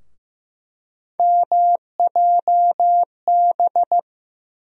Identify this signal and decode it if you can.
Morse code, heard as a keyed tone, about 15 wpm, 700 Hz. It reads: MJB
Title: USA Call Sign Postfixes - 3 Letters 15wpm